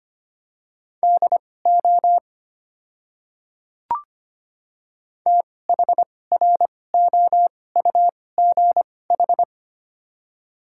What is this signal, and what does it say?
Morse code, heard as a keyed tone, about 25 wpm, spaced as for 18 wpm, 700 Hz. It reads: DO THROUGH
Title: 2x Character Spacing - Top 200 Words - 25wpm